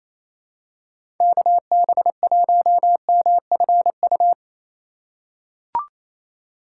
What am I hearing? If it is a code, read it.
Morse code, heard as a keyed tone, about 28 wpm, 700 Hz. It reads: KB1MFU
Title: USA Call Signs - 6 Characters - 4x Character Spacing - 28wpm